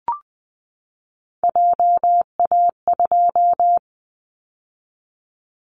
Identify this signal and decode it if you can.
Morse code, heard as a keyed tone, about 20 wpm, 700 Hz. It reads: JA2
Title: Callsign Prefixes for Top 50 DXCC - 20wpm